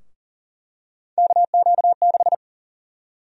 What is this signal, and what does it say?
Morse code, heard as a keyed tone, about 40 wpm, 700 Hz. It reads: KQ6
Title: USA Call Sign Prefixes - Repeated 2x - 40wpm